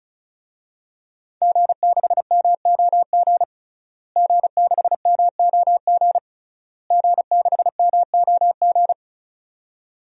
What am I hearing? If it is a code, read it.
Morse code, heard as a keyed tone, about 36 wpm, 700 Hz. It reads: G6MOG G6MOG G6MOG
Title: DXCC-100 Call Signs - 5 Characters - Repeated 3x - 35wpm